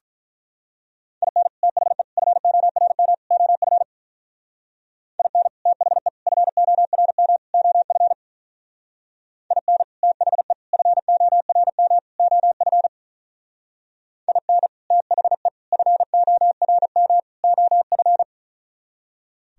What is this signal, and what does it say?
Morse code, heard as a keyed tone, about 43 wpm, 700 Hz. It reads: IN THE FORM OF IN THE FORM OF IN THE FORM OF IN THE FORM OF
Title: Speed-Racing - Sets of 4 Words 35wpm